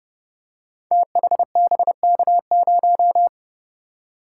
Morse code, heard as a keyed tone, about 29 wpm, 700 Hz. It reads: THBK0
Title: Code Groups - 5 Letter:Numbers - 30wpm